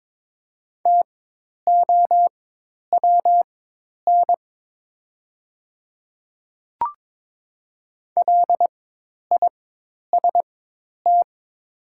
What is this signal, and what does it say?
Morse code, heard as a keyed tone, about 22 wpm, spaced as for 10 wpm, 690 Hz. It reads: TOWN LIST
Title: 4x Character Spacing - Top 500 Words - 22wpm